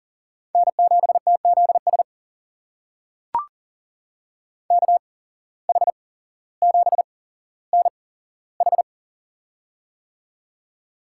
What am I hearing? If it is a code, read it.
Morse code, heard as a keyed tone, about 40 wpm, spaced as for 17 wpm, 700 Hz. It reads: N7TZS KH7NH
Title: USA Call Signs - 5 Characters - 8x Character Spacing - 40wpm